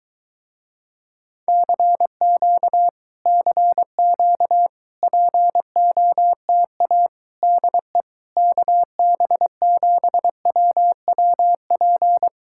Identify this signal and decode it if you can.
Morse code, heard as a keyed tone, about 22 wpm, 700 Hz. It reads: CQ CQ POTA DE KB7WWP